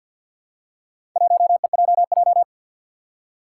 Morse code, heard as a keyed tone, about 52 wpm, 700 Hz. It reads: 1EJJ
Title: Four Challenging Letter-Number Combinations 50wpm